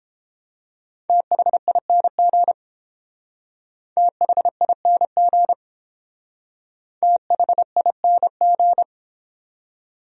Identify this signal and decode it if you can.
Morse code, heard as a keyed tone, about 29 wpm, 700 Hz. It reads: THING THING THING